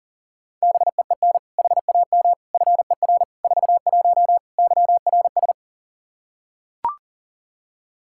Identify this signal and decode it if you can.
Morse code, heard as a keyed tone, about 40 wpm, 700 Hz. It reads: BEEN HAM FER 41 YRS
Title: K1USN SST QSO Practice 40wpm